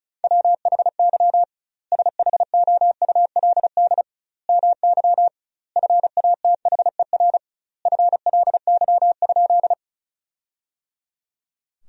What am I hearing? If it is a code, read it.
Morse code, heard as a keyed tone, about 36 wpm, 700 Hz. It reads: WHY SHOULD MY FATHER FLY?
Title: 2x Word-Spacing - Sentences from Top 500 Words - 35wpm